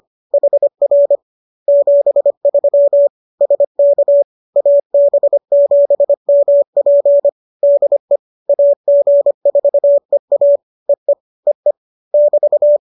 Morse code, heard as a keyed tone, about 25 wpm, 570 Hz. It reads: HR 73 SK AB7MP DE AG4EA EE EE <BT>